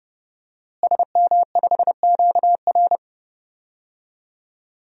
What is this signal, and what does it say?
Morse code, heard as a keyed tone, about 30 wpm, 700 Hz. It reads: SM5QR
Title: DXCC-100 Call Signs - 5 Characters - Repeated 3x - 30wpm